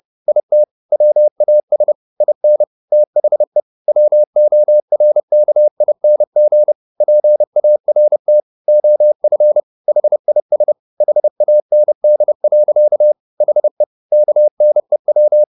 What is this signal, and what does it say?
Morse code, heard as a keyed tone, about 30 wpm, 600 Hz. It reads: IT WAS IN THE WORKING PART OF HIS HAND. HE KNEW
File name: the old man and the sea / the old man and the sea 1 30wpm